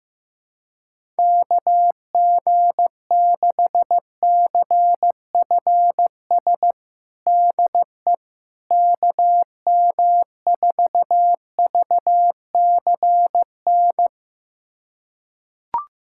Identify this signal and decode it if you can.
Morse code, heard as a keyed tone, about 15 wpm, 700 Hz. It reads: KG6CFS DE KM4VCN
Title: USA Callsign DE Callsign - 6 Characters - 15wpm